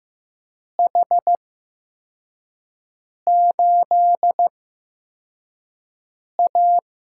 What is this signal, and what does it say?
Morse code, heard as a keyed tone, about 15 wpm, spaced as for 4 wpm, 700 Hz. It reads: H8A
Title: USA Call Signs - 4 Characters - 8x Character Spacing - 15wpm